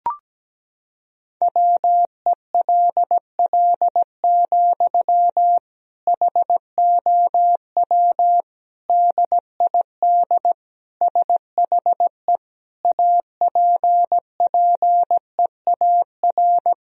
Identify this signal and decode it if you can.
Morse code, heard as a keyed tone, about 17 wpm, 700 Hz. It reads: WELL, HOW DID SHE APPEAR
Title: Sentences from Top 1000 Words - Review Part 2 - 17wpm